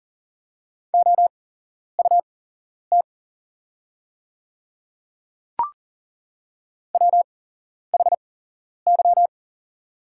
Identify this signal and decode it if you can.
Morse code, heard as a keyed tone, about 40 wpm, spaced as for 11 wpm, 700 Hz. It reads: OUT WHY
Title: 8x Character Spacing - Top 200 Words - 40wpm